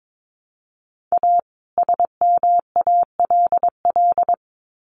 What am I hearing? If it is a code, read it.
Morse code, heard as a keyed tone, about 22 wpm, 700 Hz. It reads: A SMALL
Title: Sets of 2 Words 22wpm